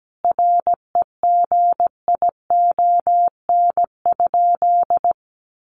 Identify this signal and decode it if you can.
Morse code, heard as a keyed tone, about 17 wpm, 700 Hz. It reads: REGION?
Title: Sentences from Top 1500 Words 17wpm